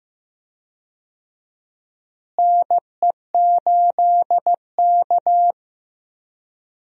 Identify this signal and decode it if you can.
Morse code, heard as a keyed tone, about 15 wpm, 700 Hz. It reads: NE8K
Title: USA Call Signs - 4 Characters - Repeated 3x - 15wpm